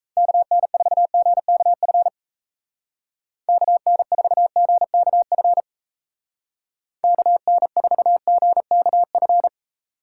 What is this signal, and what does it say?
Morse code, heard as a keyed tone, about 34 wpm, 700 Hz. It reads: KN4GKF KN4GKF KN4GKF
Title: Speed Racing - USA 2x3 Call Signs 28wpm